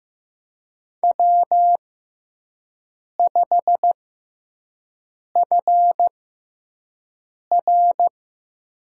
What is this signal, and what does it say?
Morse code, heard as a keyed tone, about 15 wpm, spaced as for 5 wpm, 700 Hz. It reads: W5FR
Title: USA Call Signs - 4 Characters - 6x Character Spacing - 15wpm